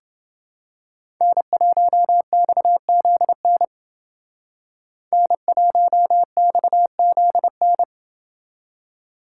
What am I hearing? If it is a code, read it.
Morse code, heard as a keyed tone, about 28 wpm, 700 Hz. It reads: N1XZN N1XZN